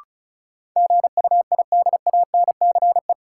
Morse code, heard as a keyed tone, about 36 wpm, 700 Hz. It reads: GUIDANCE
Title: Sentences from Top 1600 Words - New Word Drill Alt - 35wpm